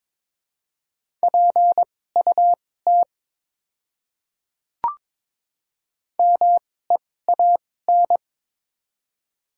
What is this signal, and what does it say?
Morse code, heard as a keyed tone, about 22 wpm, spaced as for 16 wpm, 690 Hz. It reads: PUT MEAN
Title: 2x Character Spacing - Top 200 Words - 22wpm